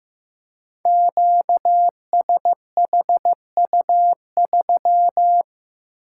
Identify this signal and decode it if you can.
Morse code, heard as a keyed tone, about 15 wpm, 700 Hz. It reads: QSHU3
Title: Code Groups - 5 Letter:Numbers - 15wpm